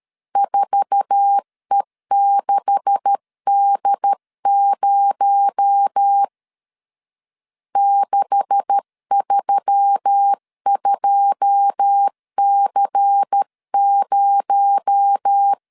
Morse code, heard as a keyed tone, about 13 wpm, 800 Hz. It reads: 4E6D0 632C0
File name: LEZIONI / 9 286 12wpm 2305